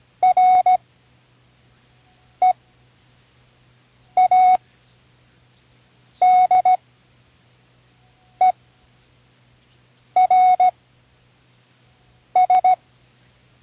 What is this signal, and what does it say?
Morse code, heard as a keyed tone, about 17 wpm, spaced as for 5 wpm, 740 Hz. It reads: READERS
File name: CW Learning Programs / Learning Morse Code (from Tune in the World) / 08 words